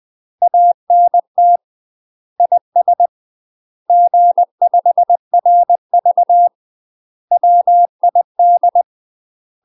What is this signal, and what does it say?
Morse code, heard as a keyed tone, about 20 wpm, 700 Hz. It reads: ANT IS G5RV WID
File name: UK MASTER QSOs TRAINING / 20WPM / UK MASTER QSOs 0007